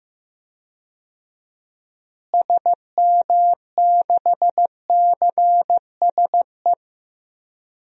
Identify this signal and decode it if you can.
Morse code, heard as a keyed tone, about 15 wpm, 700 Hz. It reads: SM6CSE
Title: DXCC-100 Call Signs - 6 Characters - Repeated 2x - 15wpm